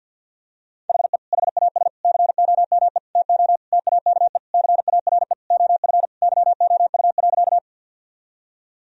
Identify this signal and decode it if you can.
Morse code, heard as a keyed tone, about 48 wpm, 700 Hz. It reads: HE HAS COME TO TAKE CARE OF YOU.
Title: Sentences from Top 1000 Words - Review Part 1 - 50wpm